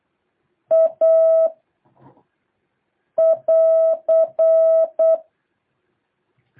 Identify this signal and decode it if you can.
Morse code, heard as a keyed tone, about 8 wpm, 650 Hz. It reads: A <AR>
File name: jh8cbh / 2023cw